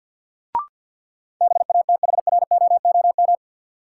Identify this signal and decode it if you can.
Morse code, heard as a keyed tone, about 52 wpm, 700 Hz. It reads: BATHROOM
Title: Sets of 5 Words Encore - New Word Drill - 50wpm